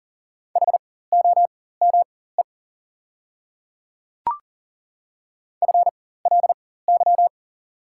Morse code, heard as a keyed tone, about 40 wpm, spaced as for 19 wpm, 690 Hz. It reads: HOME FLY
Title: 4x Character Spacing - Top 500 Words - 40wpm